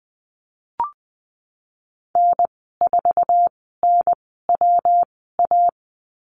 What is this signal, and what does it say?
Morse code, heard as a keyed tone, about 20 wpm, spaced as for 15 wpm, 700 Hz. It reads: N4NWA
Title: 154 Lesson 25g - Letter p - Call Signs 20wpm